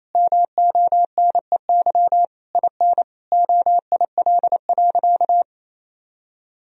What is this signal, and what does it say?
Morse code, heard as a keyed tone, about 28 wpm, 700 Hz. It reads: MONEY IN OIL.
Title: Sentences from Top 1000 Words - Review Part 4 - 28wpm